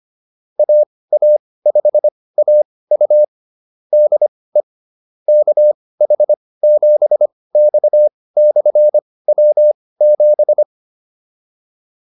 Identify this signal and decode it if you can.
Morse code, heard as a keyed tone, about 25 wpm, spaced as for 18 wpm, 600 Hz. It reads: AA5AU DE KH7X/W7
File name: QSO104 18